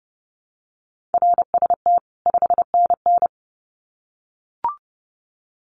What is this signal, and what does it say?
Morse code, heard as a keyed tone, about 30 wpm, 700 Hz. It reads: RST 5NN
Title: Common QSO Phrases 30wpm